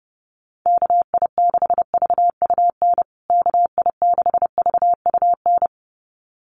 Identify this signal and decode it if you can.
Morse code, heard as a keyed tone, about 30 wpm, 700 Hz. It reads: KI6VUN KI6VUN
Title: USA Call Signs Repeated 2x 30wpm